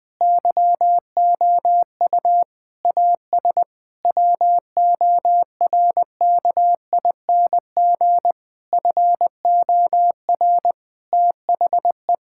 Morse code, heard as a keyed tone, about 20 wpm, 700 Hz. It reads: YOU AS WORKING FOR THE
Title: Sentences from Top 1000 Words - Review Part 6 - 20wpm